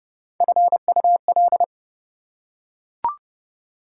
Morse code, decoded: FUL

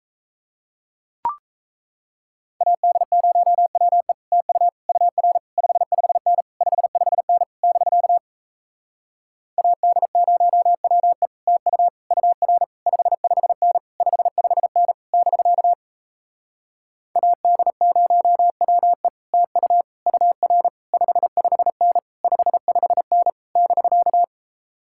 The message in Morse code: AD0WE TU UR 55N 55N <BK> AD0WE TU UR 55N 55N <BK> AD0WE TU UR 55N 55N <BK>

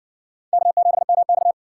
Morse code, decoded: K7MX

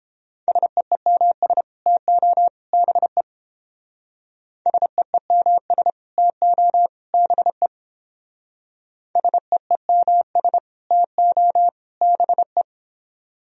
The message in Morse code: SEEMS TO BE SEEMS TO BE SEEMS TO BE

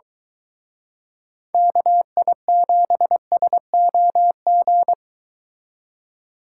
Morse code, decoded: KI7SOG

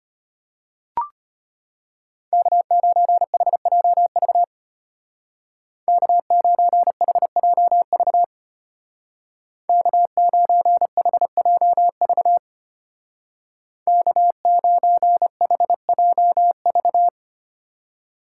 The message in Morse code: K9HJV K9HJV K9HJV K9HJV

